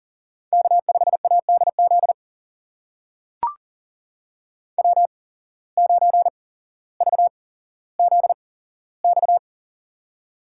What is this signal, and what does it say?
Morse code, heard as a keyed tone, about 40 wpm, spaced as for 17 wpm, 700 Hz. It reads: K5ADZ W9VZX
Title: USA Call Signs - 5 Characters - 8x Character Spacing - 40wpm